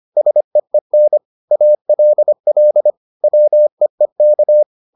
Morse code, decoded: SEEN ALL WEEK

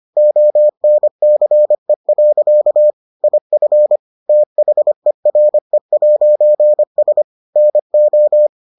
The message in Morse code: ONCE. IF THERE'S NO

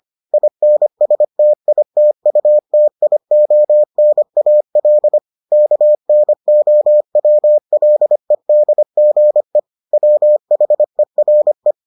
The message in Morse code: INSTITUTIONAL KNOWLEDGE WHERE

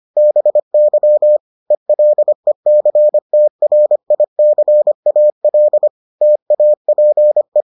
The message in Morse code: BY ELECTRICAL TAPE

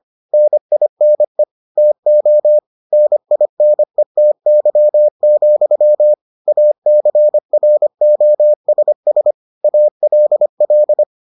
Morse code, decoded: NINE TO NINETY, ACROSS ALL

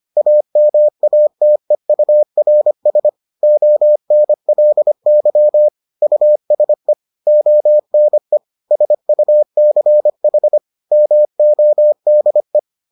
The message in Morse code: AMATEURS ONLY USE ONE SUCH MODE

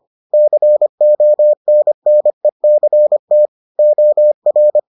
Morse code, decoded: CONNECT OR

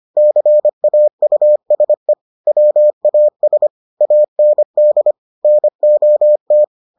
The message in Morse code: CAUSE WAS AND NOT

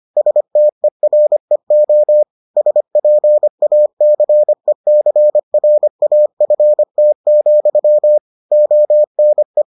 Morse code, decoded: STEREO SPACECRAFT, ONE

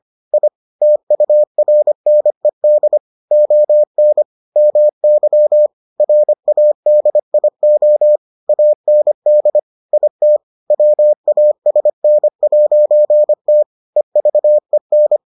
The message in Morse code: I TURNED ON MY RADIO AND IT WASN'T EVEN

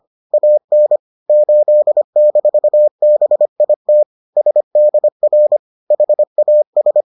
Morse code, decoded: AN 8-BIT SDR HAS

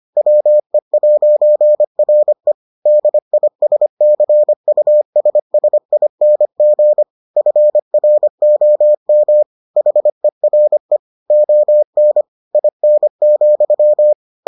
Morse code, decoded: WE'RE DISCUSSING FROM HERE ON IN,

